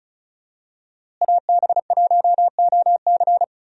Morse code, heard as a keyed tone, about 36 wpm, 700 Hz. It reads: AB1OC
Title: USA Call Signs - 5 Characters - Repeated 3x - 35wpm